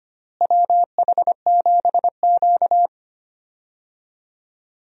Morse code, heard as a keyed tone, about 25 wpm, 700 Hz. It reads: WH7Q
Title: USA Call Signs - 4 Characters - Repeated 3x - 25wpm